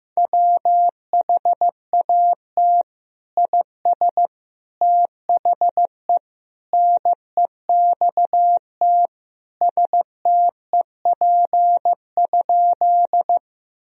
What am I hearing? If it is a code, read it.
Morse code, heard as a keyed tone, about 15 wpm, 700 Hz. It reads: WHAT IS THE NEXT STEP?
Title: Sentences from Top 1000 Words - Review Part 2 - 15wpm